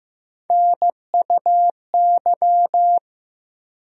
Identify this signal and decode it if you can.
Morse code, heard as a keyed tone, about 15 wpm, 700 Hz. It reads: NUY
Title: USA Call Sign Postfixes - 3 Letters 15wpm